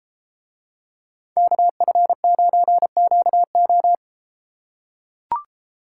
Morse code, heard as a keyed tone, about 33 wpm, 700 Hz. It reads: KF9QO